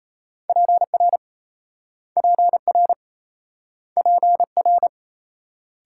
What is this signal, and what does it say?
Morse code, heard as a keyed tone, about 33 wpm, 700 Hz. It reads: PR PR PR
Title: Speed-Racing Multi-letter Phonemes - 28wpm